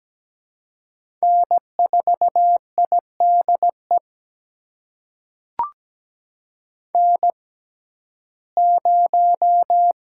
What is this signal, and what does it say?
Morse code, heard as a keyed tone, about 17 wpm, spaced as for 12 wpm, 700 Hz. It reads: N4IDE N0
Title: USA Call Signs - 5 Characters - 6x Character Spacing - 17wpm